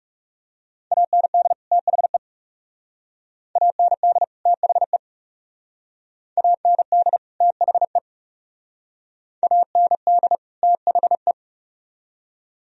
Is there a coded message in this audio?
Yes: Morse code from 0.9 to 11.3 s, heard as a keyed tone, about 40 wpm, 700 Hz. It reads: AND THE AND THE AND THE AND THE